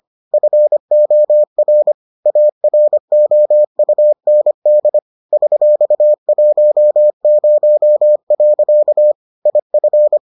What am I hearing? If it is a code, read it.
Morse code, heard as a keyed tone, about 25 wpm, 600 Hz. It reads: FOR AROUND $10. IF